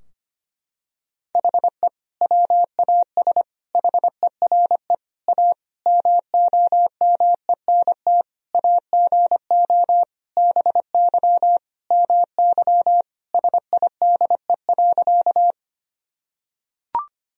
Morse code, heard as a keyed tone, about 25 wpm, 700 Hz. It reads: HE WAS HERE A MOMENT AGO BY MY SIDE.
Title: Sentences from Top 1000 Words - Review Part 3 - 25wpm